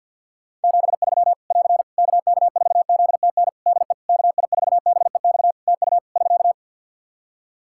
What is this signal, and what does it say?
Morse code, heard as a keyed tone, about 50 wpm, 700 Hz. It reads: 73 <AR> KK4ZTN DE KI4BEX TU <SK>
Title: K1USN SST QSO Practice 50wpm